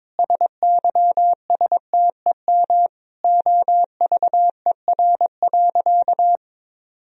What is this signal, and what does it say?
Morse code, heard as a keyed tone, about 22 wpm, 700 Hz. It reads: SYSTEM OVER.
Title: Sentences from Top 1000 Words - Review Part 3 - 22wpm